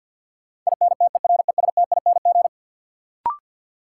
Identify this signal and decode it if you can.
Morse code, heard as a keyed tone, about 49 wpm, 700 Hz. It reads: INTERESTING
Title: Sentences from Top 1000 Words - New Word Drill - 50wpm